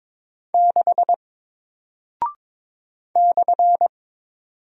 Morse code, heard as a keyed tone, about 22 wpm, 700 Hz. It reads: 6 /
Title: Single Character (Letter:Number:Punctuation:Prosign) 22wpm